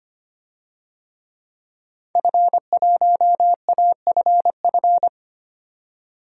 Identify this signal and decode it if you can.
Morse code, heard as a keyed tone, about 25 wpm, 700 Hz. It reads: F1AFF
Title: DXCC-100 Call Signs - 5 Characters - 25wpm